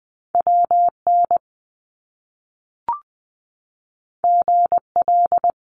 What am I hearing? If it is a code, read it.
Morse code, heard as a keyed tone, about 20 wpm, 700 Hz. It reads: WN GL
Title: Top Two Letter Combinations in Words 20wpm